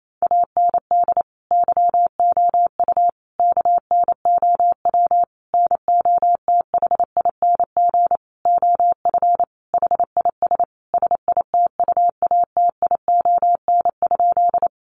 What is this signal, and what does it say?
Morse code, heard as a keyed tone, about 28 wpm, 700 Hz. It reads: AND YOU KNOW NOTHING OF HIS SITUATION?
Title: Sentences from Top 1200 Words 28wpm